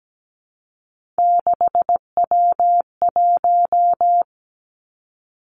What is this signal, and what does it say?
Morse code, heard as a keyed tone, about 17 wpm, 700 Hz. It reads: 6W1
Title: Three Challenging Letter-Number Combinations 17wpm